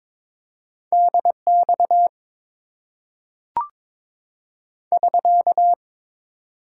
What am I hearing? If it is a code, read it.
Morse code, heard as a keyed tone, about 22 wpm, 700 Hz. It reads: DX <SK>
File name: Top 90 QSO Elements - 6x Character Spacing - 22wpm